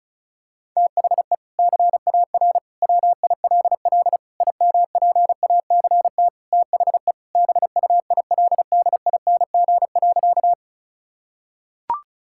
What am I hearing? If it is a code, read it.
Morse code, heard as a keyed tone, about 34 wpm, 700 Hz. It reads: THE CAR WILL IMPACT THE BUILDING.
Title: Sentences from Top 1000 Words - Review Part 6 - 35wpm